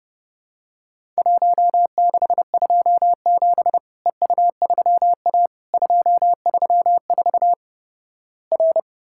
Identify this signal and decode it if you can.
Morse code, heard as a keyed tone, about 32 wpm, 700 Hz. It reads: 1627 EU3A 234 R
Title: Worked All Europe DX Contest - QTC Exchange - 30wpm